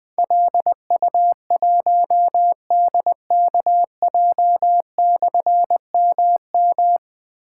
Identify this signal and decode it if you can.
Morse code, heard as a keyed tone, about 20 wpm, 700 Hz. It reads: LU1DKJ/MM
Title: Difficult Call Signs 20wpm